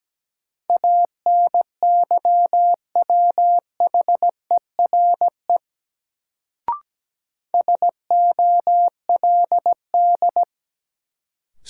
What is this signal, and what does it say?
Morse code, heard as a keyed tone, about 17 wpm, 700 Hz. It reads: ANYWHERE SOLD
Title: Sentences from Top 1400 Words - New Word Drill Alt - 17wpm